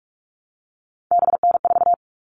Morse code, heard as a keyed tone, about 44 wpm, 700 Hz. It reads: BN4
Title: All Cut Numbers - 3-Digit 45wpm